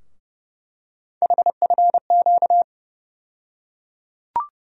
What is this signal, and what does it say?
Morse code, heard as a keyed tone, about 30 wpm, 700 Hz. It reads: HFQ
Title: USA Call Sign Postfixes - 3 Letters 30wpm